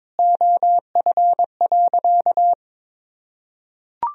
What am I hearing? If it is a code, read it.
Morse code, heard as a keyed tone, about 22 wpm, 700 Hz. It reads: OF.
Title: Sentences from Top 1000 Words - Review Part 1 - 22wpm